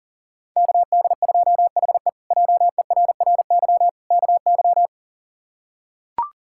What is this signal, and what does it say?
Morse code, heard as a keyed tone, about 40 wpm, 700 Hz. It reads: KD2HE JERRY KY
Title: K1USN SST QSO Practice - 40wpm